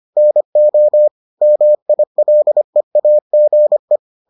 Morse code, heard as a keyed tone, about 25 wpm, 600 Hz. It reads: NO MILEAGE